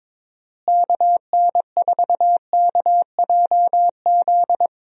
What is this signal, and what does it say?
Morse code, heard as a keyed tone, about 22 wpm, 700 Hz. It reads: KN4KJZ
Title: USA Call Signs - 6 Characters - Repeated 2x - 22wpm